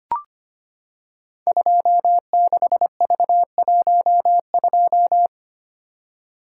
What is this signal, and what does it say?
Morse code, heard as a keyed tone, about 25 wpm, 700 Hz. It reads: 26V12
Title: Code Groups - 5 Letter:Numbers - 25wpm